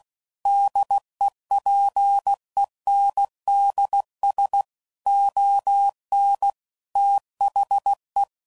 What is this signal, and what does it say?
Morse code, heard as a keyed tone, about 16 wpm, 800 Hz. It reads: DEPENDS ON THE
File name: CW Code Groups @15WPM / Technical Subjects / PwrSupply